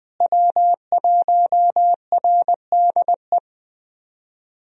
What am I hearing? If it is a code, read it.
Morse code, heard as a keyed tone, about 20 wpm, 700 Hz. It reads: W1RDE
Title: USA Call Signs - 5 Characters - Repeated 2x - 20wpm